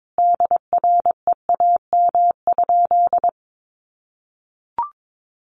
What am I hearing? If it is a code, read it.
Morse code, heard as a keyed tone, about 22 wpm, 700 Hz. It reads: DREAM?
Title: Sentences from Top 1500 Words 22wpm